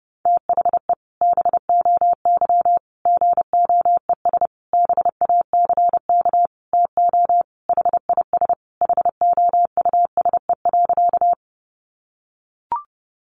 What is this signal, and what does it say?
Morse code, heard as a keyed tone, about 30 wpm, 700 Hz. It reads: THE BOY GOES BACK TO HIS HOUSE.
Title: Additional Easy Phrases 30wpm